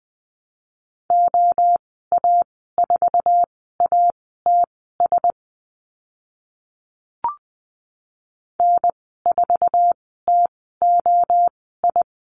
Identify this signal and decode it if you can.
Morse code, heard as a keyed tone, about 20 wpm, spaced as for 15 wpm, 700 Hz. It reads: OA4ATS N4TOI
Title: 082 Lesson 14g - Number 2 - Call Signs 20wpm